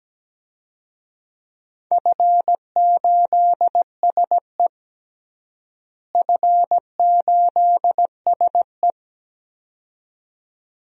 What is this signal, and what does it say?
Morse code, heard as a keyed tone, about 17 wpm, 700 Hz. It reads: F8SE F8SE
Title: DXCC-100 Call Signs - 4 Characters - Repeated 2x - 17wpm